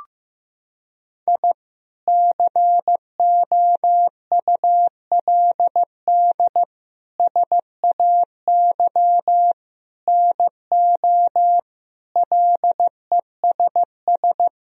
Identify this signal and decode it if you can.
Morse code, heard as a keyed tone, about 15 wpm, 700 Hz. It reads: I COULD SAY NO LESS
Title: Sentences from Top 1000 Words - Review Part 2 - 15wpm